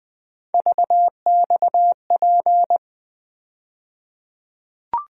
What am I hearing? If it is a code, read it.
Morse code, heard as a keyed tone, about 20 wpm, 700 Hz. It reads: VXP
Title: USA Call Sign Postfixes - 3 Letters 20wpm